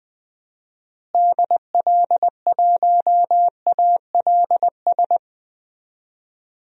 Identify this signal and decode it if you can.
Morse code, heard as a keyed tone, about 20 wpm, 700 Hz. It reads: DL1ALS
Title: DXCC-100 Call Signs - 6 Characters - Repeated 3x - 20wpm